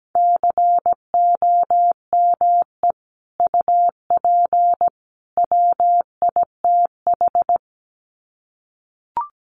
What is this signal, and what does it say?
Morse code, heard as a keyed tone, about 17 wpm, 700 Hz. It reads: COME UP WITH
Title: Sets of 3 Words 17wpm